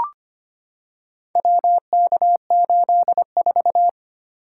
Morse code, heard as a keyed tone, about 25 wpm, 700 Hz. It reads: WK84
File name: practice / 25wpm / Four Letter-Number Combinations - 25wpm